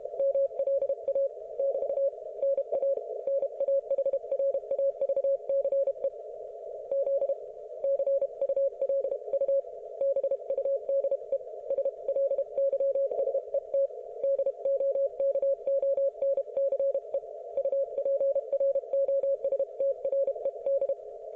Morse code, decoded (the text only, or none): MLA <BT> NA NAHRAVCE Z CULU BUDE SLYSET DOKONCE UPROSTRED